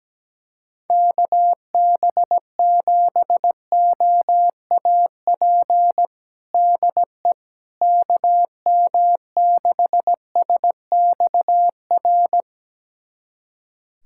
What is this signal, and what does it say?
Morse code, heard as a keyed tone, about 17 wpm, 700 Hz. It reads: KB7OAP DE KM6SXR
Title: USA Callsign DE Callsign - 6 Characters - 17wpm